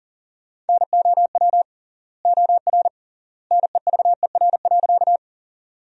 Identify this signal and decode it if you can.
Morse code, heard as a keyed tone, about 37 wpm, 700 Hz. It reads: NOW OR NEVER.
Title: 3x Word-Spacing - Sentences from Top 300 Words - 40wpm